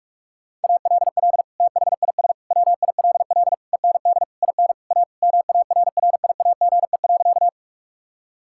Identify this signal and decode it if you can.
Morse code, heard as a keyed tone, about 46 wpm, 700 Hz. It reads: ALL THIS WILL END IN A MARRIAGE.
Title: Sentences from Top 1000 Words - Review Part 5 - 45wpm